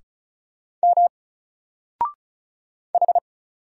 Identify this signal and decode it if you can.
Morse code, heard as a keyed tone, about 33 wpm, 700 Hz. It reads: M H
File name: Single-Letter-Number-35wpm